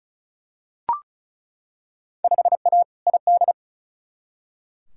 Morse code, heard as a keyed tone, about 34 wpm, 700 Hz. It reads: 5A ID
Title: Field Day Exchanges Encore - 35wpm